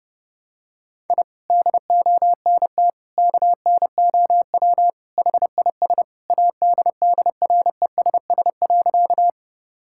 Morse code, heard as a keyed tone, about 30 wpm, 700 Hz. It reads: I DONT KNOW HIS ADDRESS.
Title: Sentences from Top 1000 Words - Review Part 6 - 30wpm